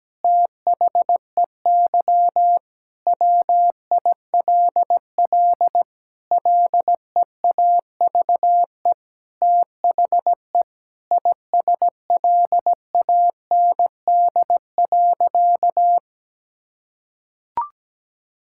Morse code, heard as a keyed tone, about 17 wpm, 700 Hz. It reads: THEY WILL LEAVE THE ISLAND.
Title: Sentences from Top 1000 Words - Review Part 6 - 17wpm